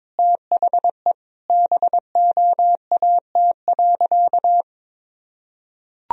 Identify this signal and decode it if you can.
Morse code, heard as a keyed tone, about 22 wpm, 700 Hz. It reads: THE BOAT.
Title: Sentences from Top 1000 Words - Review Part 4 - 22wpm